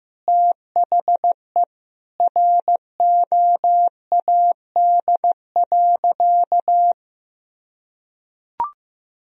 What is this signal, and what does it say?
Morse code, heard as a keyed tone, about 15 wpm, 700 Hz. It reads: THE ROAD.
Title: Sentences from Top 400 Words and Abbreviations 15wpm